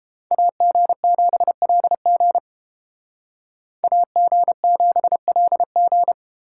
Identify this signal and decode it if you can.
Morse code, heard as a keyed tone, about 30 wpm, 700 Hz. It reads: AG7LG AG7LG